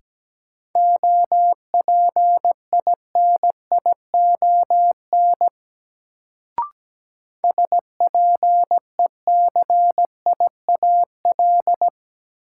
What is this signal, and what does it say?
Morse code, heard as a keyed tone, about 17 wpm, 700 Hz. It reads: OPINION SPECIAL
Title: Sentences from Top 1200 Words - New Word Drill Alt - 17wpm